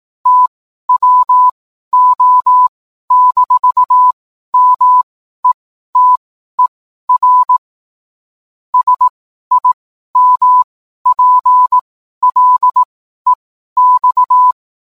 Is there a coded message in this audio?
Yes: Morse code from 0.3 to 14.5 s, heard as a keyed tone, about 18 wpm, spaced as for 12 wpm, 1000 Hz. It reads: TWO-METER SIMPLEX